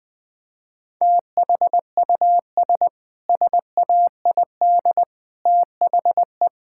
Morse code, heard as a keyed tone, about 21 wpm, 700 Hz. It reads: THUS SAID THE
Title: Sentences from Top 1000 Words - Review Part 5 - 20wpm